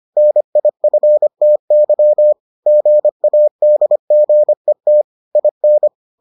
Morse code, heard as a keyed tone, about 25 wpm, 600 Hz. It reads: NIFTY GADGET IN